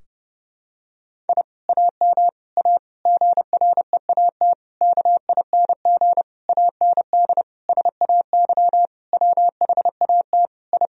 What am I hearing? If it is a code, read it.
Morse code, heard as a keyed tone, about 30 wpm, 700 Hz. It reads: I AM A GREAT KING AND SAY WHAT I
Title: Sentences from Top 1000 Words - Review Part 4 - 30wpm